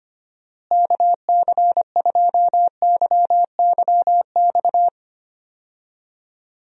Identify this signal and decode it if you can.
Morse code, heard as a keyed tone, about 25 wpm, 700 Hz. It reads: KC2YYX
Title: USA Call Signs - 6 Characters - Repeated 2x - 25wpm